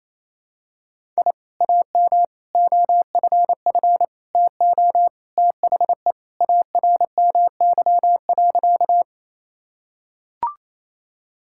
Morse code, heard as a keyed tone, about 28 wpm, 700 Hz. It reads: I AM OFF TO THE ARMY.
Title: Sentences from Top 1000 Words - Review Part 5 - 28wpm